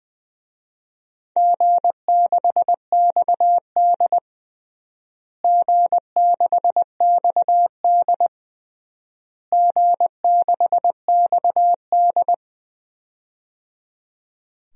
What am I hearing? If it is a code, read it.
Morse code, heard as a keyed tone, about 20 wpm, 700 Hz. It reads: G6XD G6XD G6XD
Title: DXCC-100 Call Signs - 4 Characters - Repeated 3x - 20wpm